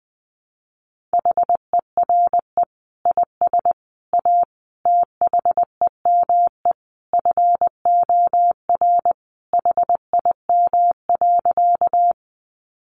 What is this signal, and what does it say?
Morse code, heard as a keyed tone, about 20 wpm, 700 Hz. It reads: HERE IS A THEME FOR HIM.
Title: Sentences from Top 1100 Words 20wpm